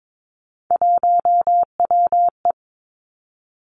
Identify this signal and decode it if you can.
Morse code, heard as a keyed tone, about 22 wpm, 700 Hz. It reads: 1WE
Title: Three Challenging Letter-Number Combinations 22wpm